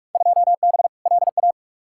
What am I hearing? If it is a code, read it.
Morse code, heard as a keyed tone, about 42 wpm, 700 Hz. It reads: 2B LA
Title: Field Day Exchanges Encore - 45wpm